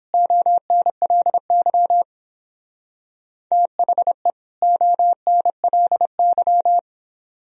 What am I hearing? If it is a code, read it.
Morse code, heard as a keyed tone, about 28 wpm, 700 Hz. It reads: ONLY THE ONLY